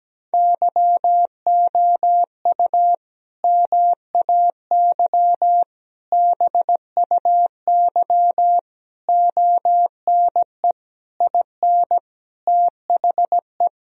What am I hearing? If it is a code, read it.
Morse code, heard as a keyed tone, about 17 wpm, 700 Hz. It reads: YOU MAY BUY ONE IN THE
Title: Sentences from Top 1000 Words - Review Part 4 - 17wpm